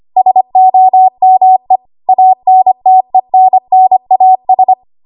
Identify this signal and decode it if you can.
Morse code, heard as a keyed tone, about 25 wpm, 750 Hz. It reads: SOME ANTENNAS
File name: Archive / 25 WPM / 25WPM